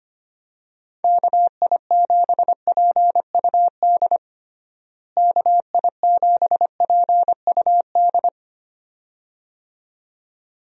Morse code, heard as a keyed tone, about 25 wpm, 700 Hz. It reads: KI7PUD KI7PUD
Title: USA Call Signs - 6 Characters - Repeated 2x - 25wpm